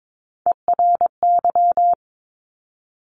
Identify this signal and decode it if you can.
Morse code, heard as a keyed tone, about 22 wpm, 700 Hz. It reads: ERY
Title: Top Three Letter Combinations in Words 22wpm